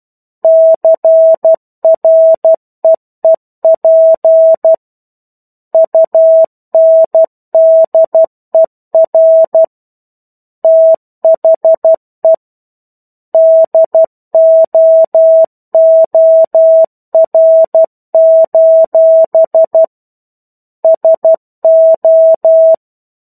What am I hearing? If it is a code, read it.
Morse code, heard as a keyed tone, about 12 wpm, 650 Hz. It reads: CREEP UNDER THE DOOR: SO
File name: 01-Chapter-01 12 WPM